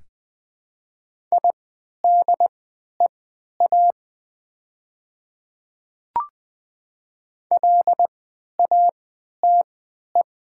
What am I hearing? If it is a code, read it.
Morse code, heard as a keyed tone, about 20 wpm, spaced as for 11 wpm, 700 Hz. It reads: IDEA LATE
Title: 3x Character Spacing - Top 300 Words - 20wpm